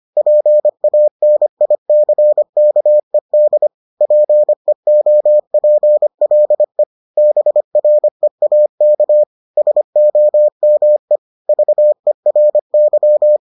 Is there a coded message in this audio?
Yes: Morse code from 0.2 to 13.4 s, heard as a keyed tone, about 25 wpm, 600 Hz. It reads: PANICKED PEOPLE BREAK SOME VERY